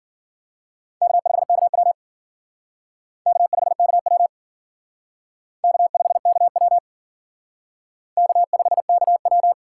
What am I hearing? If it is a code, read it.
Morse code, heard as a keyed tone, about 50 wpm, 700 Hz. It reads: K5KW K5KW K5KW K5KW
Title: Speed Racing - USA 1x2 Call Signs 40wpm